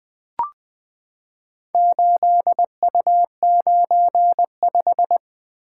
Morse code, heard as a keyed tone, about 20 wpm, 700 Hz. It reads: 8U95
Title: Four Letter-Number Combinations - 20wpm